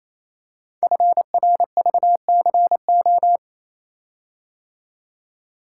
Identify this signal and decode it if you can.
Morse code, heard as a keyed tone, about 28 wpm, 700 Hz. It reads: FRVCO
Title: Code Groups - 5 Letters - 28wpm